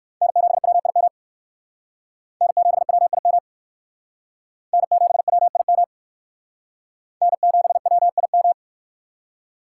Morse code, heard as a keyed tone, about 58 wpm, 700 Hz. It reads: N7WIM N7WIM N7WIM N7WIM